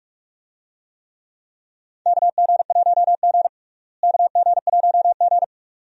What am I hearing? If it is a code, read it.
Morse code, heard as a keyed tone, about 44 wpm, 700 Hz. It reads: KG1G KG1G
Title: USA Call Signs - 4 Characters - Repeated 2x - 45wpm